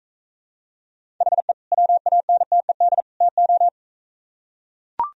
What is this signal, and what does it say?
Morse code, heard as a keyed tone, about 43 wpm, 700 Hz. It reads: HE WANTED TO